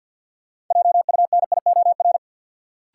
Morse code, heard as a keyed tone, about 50 wpm, 700 Hz. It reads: JUNIOR
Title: Sentences from Top 1000 Words - Review Part 7 - New Word Drill - 50wpm